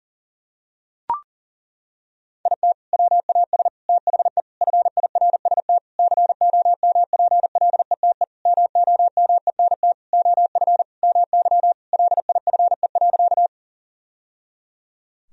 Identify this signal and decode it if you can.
Morse code, heard as a keyed tone, about 40 wpm, 700 Hz. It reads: IT WAS THE FIRST COMPLETE MOMENT OF MY LIFE.
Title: Sentences from Top 1000 Words - Review Part 2 - 40wpm